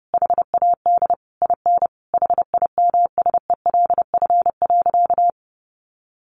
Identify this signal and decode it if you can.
Morse code, heard as a keyed tone, about 30 wpm, 700 Hz. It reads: HAD IN HIMSELF.
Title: Sentences from Top 1500 Words 30wpm